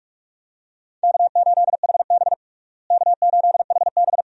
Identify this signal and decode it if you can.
Morse code, heard as a keyed tone, about 41 wpm, 700 Hz. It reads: K8HB K8HB
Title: USA Call Signs - 4 Characters - Repeated 2x - 45wpm